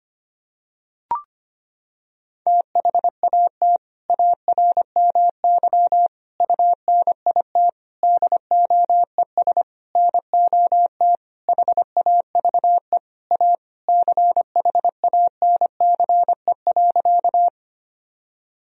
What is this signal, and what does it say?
Morse code, heard as a keyed tone, about 25 wpm, 700 Hz. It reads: THAT ARMY UNIT DOES NOT HAVE A CHANCE.
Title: Sentences from Top 1000 Words - Review Part 2 - 25wpm